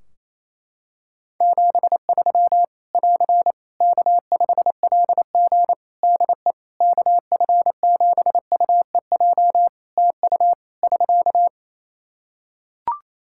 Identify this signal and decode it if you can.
Morse code, heard as a keyed tone, about 28 wpm, 700 Hz. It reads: 73 <AR> K5LG DE KF7UEJ TU <SK>
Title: K1USN SST QSO Practice 28wpm